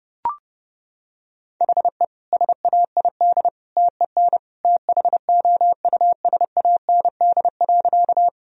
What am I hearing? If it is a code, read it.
Morse code, heard as a keyed tone, about 30 wpm, 700 Hz. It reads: HE SAID TEN THOUSAND.
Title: Sentences from Top 1000 Words - Review Part 3 - 30wpm